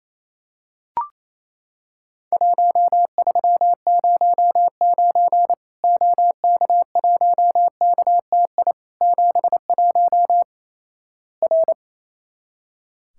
Morse code, heard as a keyed tone, about 28 wpm, 700 Hz. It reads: 1309 OK1KTI 71 R
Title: Worked All Europe DX Contest - QTC Exchange - 28wpm